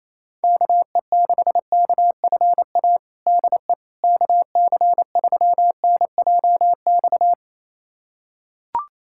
Morse code, heard as a keyed tone, about 28 wpm, 700 Hz. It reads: KE6KFA DE KC3NJX
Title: USA Callsign DE Callsign - 6 Characters - 28wpm